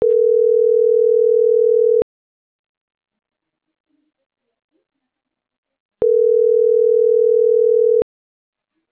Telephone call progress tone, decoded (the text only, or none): ringback tone